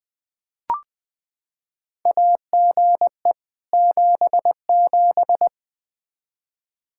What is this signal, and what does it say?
Morse code, heard as a keyed tone, about 21 wpm, 700 Hz. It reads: AGE 77
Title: Practice QSO - Exchange 3 - More Info - 20wpm